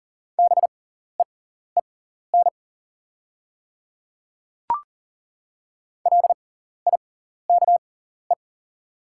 Morse code, heard as a keyed tone, about 38 wpm, spaced as for 14 wpm, 700 Hz. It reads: BEEN LIKE
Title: 6x Character Spacing - Top 200 Words - 40wpm